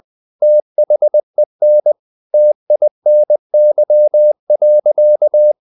Morse code, heard as a keyed tone, about 20 wpm, 600 Hz. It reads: THEN TINY.
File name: the old man and the sea part 2 20wpm